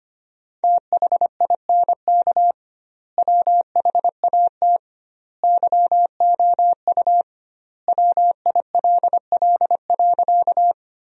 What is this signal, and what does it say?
Morse code, heard as a keyed tone, about 25 wpm, 700 Hz. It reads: THINK WHAT YOU WILL.
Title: 2x Word-Spacing - Sentences from Top 200 Words - 25wpm